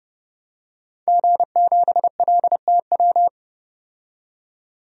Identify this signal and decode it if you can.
Morse code, heard as a keyed tone, about 32 wpm, 700 Hz. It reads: G7LTW
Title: DXCC-100 Call Signs - 5 Characters - 30wpm